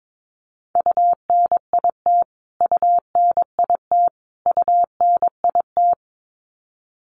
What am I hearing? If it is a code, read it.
Morse code, heard as a keyed tone, about 22 wpm, 700 Hz. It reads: UNIT UNIT UNIT
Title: Top 500 Words Repeated 22wpm